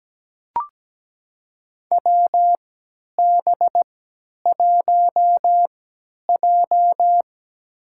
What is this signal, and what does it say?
Morse code, heard as a keyed tone, about 17 wpm, spaced as for 10 wpm, 700 Hz. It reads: WB1J
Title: USA Call Signs - 4 Characters - 3x Character Spacing - 17wpm